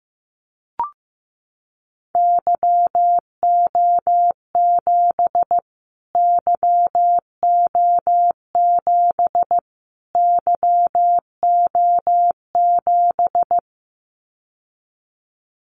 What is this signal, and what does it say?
Morse code, heard as a keyed tone, about 15 wpm, 700 Hz. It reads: YO7 YO7 YO7
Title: Callsign Prefixes for Top 25 DXCC - Repeated 3x - 15wpm